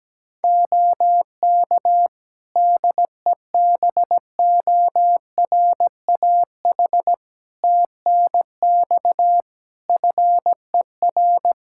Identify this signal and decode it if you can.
Morse code, heard as a keyed tone, about 17 wpm, 700 Hz. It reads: OK DEBORAH TNX FER
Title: Practice QSO - Exchange 4 - Ending - 17wpm